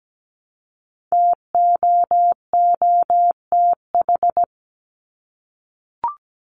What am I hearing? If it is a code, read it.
Morse code, heard as a keyed tone, about 17 wpm, 700 Hz. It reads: TOOTH
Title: Top 1751-2000 Words 17wpm